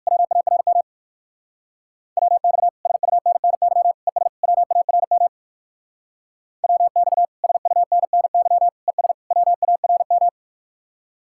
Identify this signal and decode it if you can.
Morse code, heard as a keyed tone, about 54 wpm, 700 Hz. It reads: WARM WX SUNNY ES WARM WX SUNNY ES WARM
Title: Speed-Racing Common QSO Phrases - 45wpm